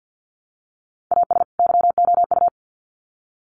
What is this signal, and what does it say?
Morse code, heard as a keyed tone, about 50 wpm, 700 Hz. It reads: AS YOU
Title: Sets of 2 Words 50wpm